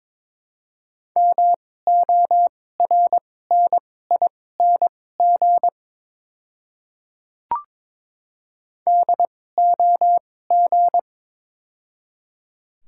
Morse code, heard as a keyed tone, about 22 wpm, spaced as for 16 wpm, 700 Hz. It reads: MORNING DOG
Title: 2x Character Spacing - Top 500 Words - 22wpm